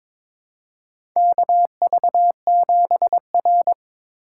Morse code, heard as a keyed tone, about 22 wpm, 700 Hz. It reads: KV7R
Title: USA Call Signs - 4 Characters - 4x Character Spacing - 22wpm